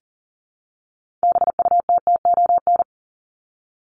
Morse code, heard as a keyed tone, about 40 wpm, 700 Hz. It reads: BUTTON
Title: Top 1001-1250 Words 40wpm